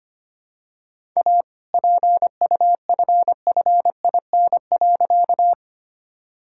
Morse code, heard as a keyed tone, about 25 wpm, 700 Hz. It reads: A PUFFIN.
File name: practice / 25wpm / All-Age Appropriate Jokes - 25wpm